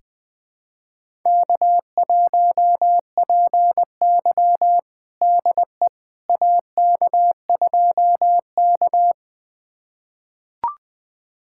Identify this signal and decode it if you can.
Morse code, heard as a keyed tone, about 21 wpm, 700 Hz. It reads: K1PY DE AK2K
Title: USA Callsign DE Callsign - 4 Characters - 20wpm